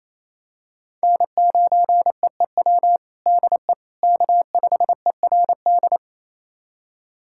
Morse code, heard as a keyed tone, about 28 wpm, 700 Hz. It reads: N9EEW DE K5ERD
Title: USA Callsign DE Callsign - 5 Characters - 28wpm